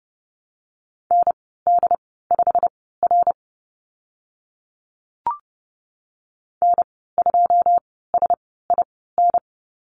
Morse code, heard as a keyed tone, about 30 wpm, spaced as for 17 wpm, 700 Hz. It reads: ND5R N2SIN
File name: course / 30:10wpm / 088 Lesson 15g - Number 5 - Call Signs 30wpm